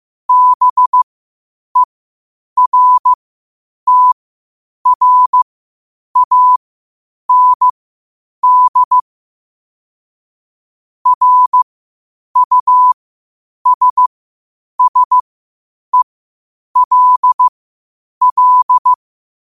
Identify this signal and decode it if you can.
Morse code, heard as a keyed tone, about 15 wpm, spaced as for 8 wpm, 1000 Hz. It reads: BERTRAND RUSSELL
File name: qotdfiles / 8wpm